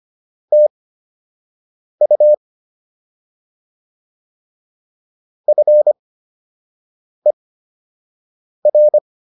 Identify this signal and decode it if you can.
Morse code, heard as a keyed tone, about 25 wpm, spaced as for 6 wpm, 600 Hz. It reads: TU FER
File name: QSO 105 06